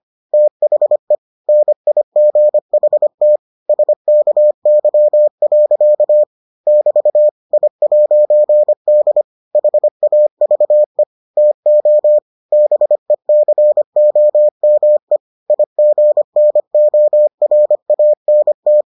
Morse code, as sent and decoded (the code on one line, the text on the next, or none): - .... / -. .. --. .... - / ... -.- -.-- .-.-.- / -...- / .. .----. -.. / .... .- ...- . / - --- / -... -.-. --- -- . / .. --. -. --- .-. .- -. -
THE NIGHT SKY. <BT> I'D HAVE TO BECOME IGNORANT